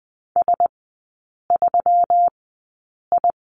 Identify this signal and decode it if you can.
Morse code, heard as a keyed tone, about 20 wpm, spaced as for 8 wpm, 700 Hz. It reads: S3I
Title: 138 Lesson 23e - Letter f - Three Characters 20wpm